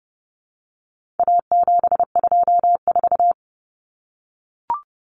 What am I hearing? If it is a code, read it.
Morse code, heard as a keyed tone, about 30 wpm, 700 Hz. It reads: A724
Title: Common Cut Numbers - 4-Digit 30wpm